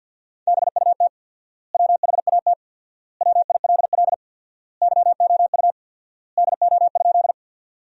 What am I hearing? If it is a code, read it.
Morse code, heard as a keyed tone, about 53 wpm, 700 Hz. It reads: BUT WHAT WILL YOU DO?
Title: 4x Word-Spacing - Sentences from Top 100 Words - 50wpm